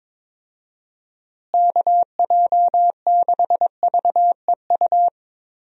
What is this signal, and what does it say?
Morse code, heard as a keyed tone, about 21 wpm, 700 Hz. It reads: KJ6VEU
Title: USA Call Signs - 6 Characters - Repeated 3x - 22wpm